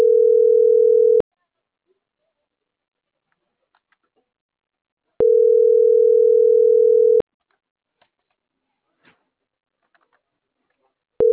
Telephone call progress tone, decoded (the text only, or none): ringback tone